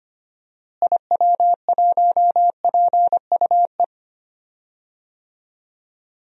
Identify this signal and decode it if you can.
Morse code, heard as a keyed tone, about 25 wpm, 700 Hz. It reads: IW1PUE
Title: DXCC-100 Call Signs - 6 Characters - Repeated 3x - 25wpm